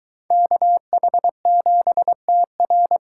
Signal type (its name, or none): Morse code